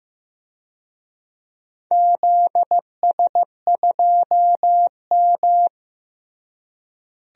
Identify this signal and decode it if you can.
Morse code, heard as a keyed tone, about 15 wpm, 700 Hz. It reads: ZS2M
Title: DXCC-100 Call Signs - 4 Characters - Repeated 3x - 15wpm